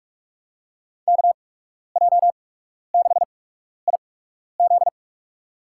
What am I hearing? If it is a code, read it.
Morse code, heard as a keyed tone, about 46 wpm, spaced as for 12 wpm, 700 Hz. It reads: KJ6IZ
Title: USA Call Signs - 5 Characters - 8x Character Spacing - 45wpm